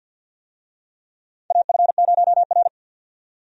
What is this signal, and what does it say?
Morse code, heard as a keyed tone, about 47 wpm, 700 Hz. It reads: AF0R
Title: USA Call Signs - 4 Characters - 50wpm